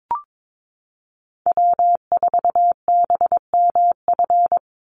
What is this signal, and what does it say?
Morse code, heard as a keyed tone, about 22 wpm, 700 Hz. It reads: W4BMF
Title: USA Call Signs 22wpm